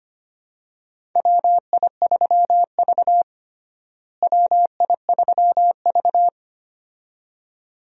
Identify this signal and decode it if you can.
Morse code, heard as a keyed tone, about 25 wpm, 700 Hz. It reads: WI3V WI3V
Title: USA Call Signs - 4 Characters - Repeated 2x - 25wpm